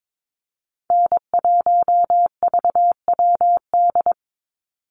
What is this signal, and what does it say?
Morse code, heard as a keyed tone, about 22 wpm, 700 Hz. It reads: N1VWD
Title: NA SOTA Chaser Callsigns 22wpm